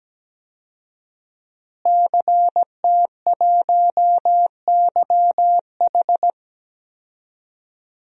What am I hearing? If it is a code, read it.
Morse code, heard as a keyed tone, about 17 wpm, 700 Hz. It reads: CT1YH
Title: DXCC-100 Call Signs - 5 Characters - Repeated 2x - 17wpm